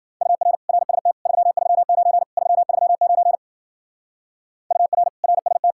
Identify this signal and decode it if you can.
Morse code, heard as a keyed tone, about 48 wpm, 700 Hz. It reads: UR RST 339 339 UR RST
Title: Speed-Racing Common QSO Phrases Encore - 40wpm